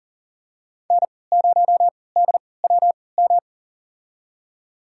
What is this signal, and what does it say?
Morse code, heard as a keyed tone, about 38 wpm, spaced as for 22 wpm, 700 Hz. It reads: N0DWM
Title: USA Call Signs - 5 Characters - 3x Character Spacing - 40wpm